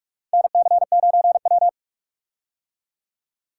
Morse code, heard as a keyed tone, about 46 wpm, 700 Hz. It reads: NC9W
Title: K1USN SST QSO Practice - 45wpm